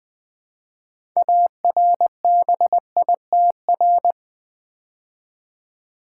Morse code, heard as a keyed tone, about 20 wpm, 700 Hz. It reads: ARBITR
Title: Build A Word - 3 Letters At A Time - 9-Letter Words - 20wpm